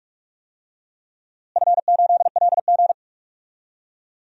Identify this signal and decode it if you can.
Morse code, heard as a keyed tone, about 41 wpm, 700 Hz. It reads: F8LG
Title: DXCC-100 Call Signs - 4 Characters - Repeated 2x - 45wpm